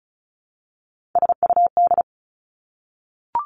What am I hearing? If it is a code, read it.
Morse code, heard as a keyed tone, about 36 wpm, 700 Hz. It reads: SUD
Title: Three Challenging Letter-Number Combinations 35wpm